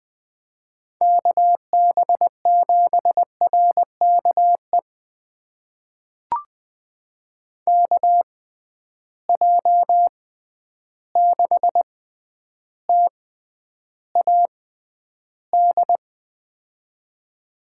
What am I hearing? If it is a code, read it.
Morse code, heard as a keyed tone, about 20 wpm, spaced as for 10 wpm, 700 Hz. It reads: KB7RKE KJ6TAD
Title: USA Call Signs - 6 Characters - 6x Character Spacing - 20wpm